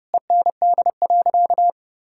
Morse code, heard as a keyed tone, about 32 wpm, 700 Hz. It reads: END.
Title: Sentences from Top 1000 Words - Review Part 5 - 30wpm